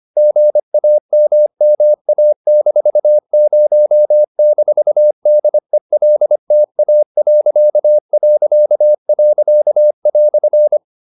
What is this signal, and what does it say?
Morse code, heard as a keyed tone, about 25 wpm, 600 Hz. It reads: GAMMA-0-DELTA..."